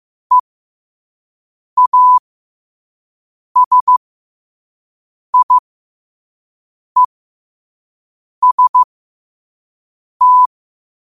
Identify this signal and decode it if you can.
Morse code, heard as a keyed tone, about 15 wpm, spaced as for 5 wpm, 1000 Hz. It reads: EASIEST
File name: qotdfiles / 5wpm